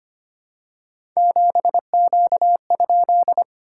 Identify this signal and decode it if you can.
Morse code, heard as a keyed tone, about 25 wpm, 700 Hz. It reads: 7Q?
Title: SOTA QSO Practice 25wpm